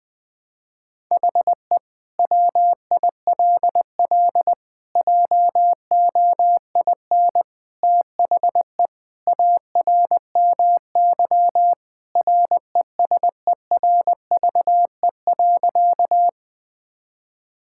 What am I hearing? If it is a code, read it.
Morse code, heard as a keyed tone, about 20 wpm, 700 Hz. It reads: HE WILL JOIN THE ARMY RESERVE.
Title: Sentences from Top 1000 Words - Review Part 6 - 20wpm